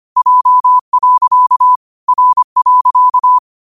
Morse code, heard as a keyed tone, about 25 wpm, 1000 Hz. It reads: J. R.
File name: qotdfiles / 25wpm